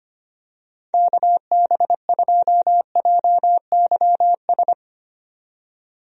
Morse code, heard as a keyed tone, about 25 wpm, 700 Hz. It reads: KB2JYS
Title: USA Call Signs - 6 Characters - 25wpm